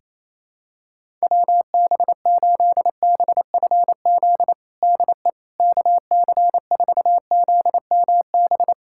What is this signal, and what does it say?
Morse code, heard as a keyed tone, about 27 wpm, 700 Hz. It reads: WB8BFZ DE KC4ZMB
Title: USA Callsign DE Callsign - 6 Characters - 28wpm